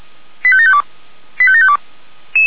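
Five-tone selective call (ZVEI), heard as quick runs of five tones, tones of about 70 ms. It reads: 85752 85752
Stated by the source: ZVEI five-tone selective call received by SDR.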